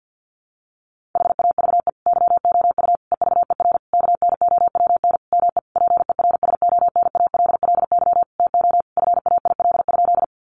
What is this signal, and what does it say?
Morse code, heard as a keyed tone, about 50 wpm, 700 Hz. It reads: HAVE YOU EVER KNOWN ME PERSONALLY TO FAIL?
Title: Sentences from Top 1400 Words 50wpm